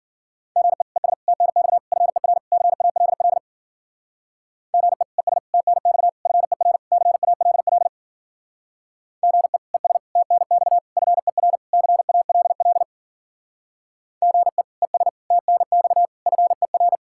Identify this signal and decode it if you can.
Morse code, heard as a keyed tone, about 53 wpm, 700 Hz. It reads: GE ES TNX FER CALL GE ES TNX FER CALL GE ES TNX FER CALL GE ES TNX FER